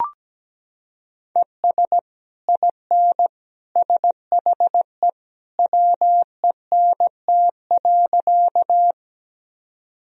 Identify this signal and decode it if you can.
Morse code, heard as a keyed tone, about 17 wpm, 700 Hz. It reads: ES IN SHE WENT.
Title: Sentences from Top 200 Words and Abbreviations 17wpm